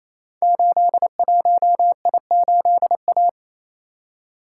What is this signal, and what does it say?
Morse code, heard as a keyed tone, about 28 wpm, 700 Hz. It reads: 81I8A
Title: Code Groups - 5 Letter:Numbers - 28wpm